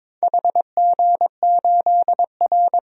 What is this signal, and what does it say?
Morse code, heard as a keyed tone, about 22 wpm, 700 Hz. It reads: HG8R
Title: DXCC-100 Call Signs - 4 Characters - 22wpm